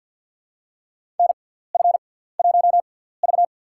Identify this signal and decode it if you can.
Morse code, heard as a keyed tone, about 48 wpm, spaced as for 17 wpm, 700 Hz. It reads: NF1V
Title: USA Call Signs - 6x Character Spacing - 50wpm